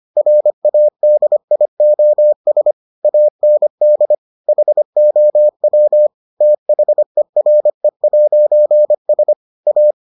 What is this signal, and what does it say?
Morse code, heard as a keyed tone, about 25 wpm, 600 Hz. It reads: RADIOS AND HOW THERE'S A